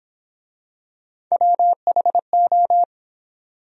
Morse code, heard as a keyed tone, about 28 wpm, 700 Hz. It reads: WHO